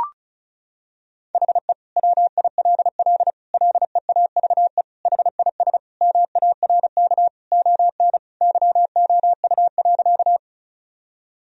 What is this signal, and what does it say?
Morse code, heard as a keyed tone, about 34 wpm, 700 Hz. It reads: HE WILL LEAVE HIS MARK ON YOU.
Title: Sentences from Top 1000 Words - Review Part 1 - 35wpm